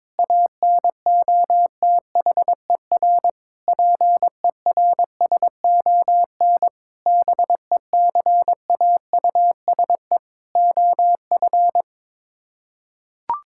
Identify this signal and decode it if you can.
Morse code, heard as a keyed tone, about 21 wpm, 700 Hz. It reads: ANOTHER PERSON BECAUSE OF
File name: Sets of 9 Words - 22wpm